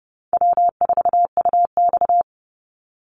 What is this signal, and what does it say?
Morse code, heard as a keyed tone, about 30 wpm, 700 Hz. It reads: W4UX
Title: Easy Call Signs - 30wpm